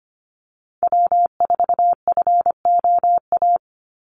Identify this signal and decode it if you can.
Morse code, heard as a keyed tone, about 25 wpm, 700 Hz. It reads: W4FOA
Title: Call Signs - 25wpm